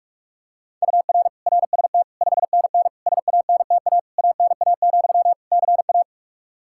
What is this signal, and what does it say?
Morse code, heard as a keyed tone, about 46 wpm, 700 Hz. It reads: UR RST 5NN SANTA ANA, CA
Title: Practice QSO - Exchange 1 - Beginning - 45wpm